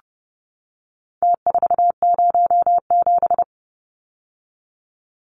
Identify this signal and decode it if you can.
Morse code, heard as a keyed tone, about 30 wpm, 700 Hz. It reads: T407
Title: Numbers - Contest Style - 4-Digit 30wpm